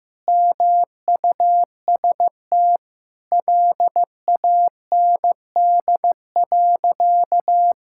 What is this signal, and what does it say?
Morse code, heard as a keyed tone, about 15 wpm, 700 Hz. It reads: MUST LAND.
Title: Sentences from Top 1000 Words - Review Part 6 - 15wpm